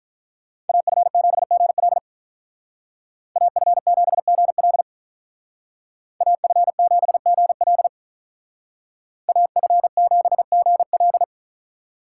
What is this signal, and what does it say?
Morse code, heard as a keyed tone, about 43 wpm, 700 Hz. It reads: AF7GL AF7GL AF7GL AF7GL
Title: Speed Racing - USA 2x2 Call Signs 35wpm